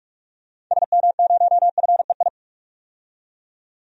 Morse code, heard as a keyed tone, about 41 wpm, 700 Hz. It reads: SM0FEI
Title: DXCC-100 Call Signs - 6 Characters - 45wpm